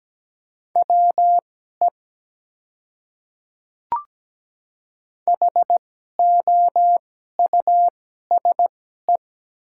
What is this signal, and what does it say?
Morse code, heard as a keyed tone, about 17 wpm, spaced as for 12 wpm, 700 Hz. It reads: WE HOUSE
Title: 2x Character Spacing - Top 300 Words - 17wpm